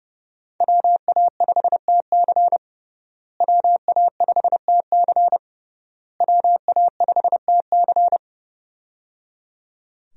Morse code, heard as a keyed tone, about 30 wpm, 700 Hz. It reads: WA5TC WA5TC WA5TC
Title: USA Call Signs - 5 Characters - Repeated 3x - 30wpm